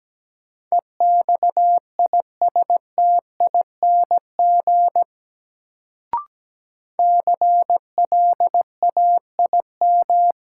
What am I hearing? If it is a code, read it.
Morse code, heard as a keyed tone, about 17 wpm, 700 Hz. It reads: EXISTING CLAIM
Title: Sentences from Top 1500 Words - New Word Drill Alt - 17wpm